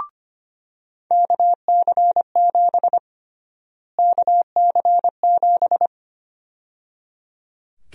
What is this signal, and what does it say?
Morse code, heard as a keyed tone, about 25 wpm, 700 Hz. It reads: KC7 KC7
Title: USA Call Sign Prefixes - Repeated 2x - 25wpm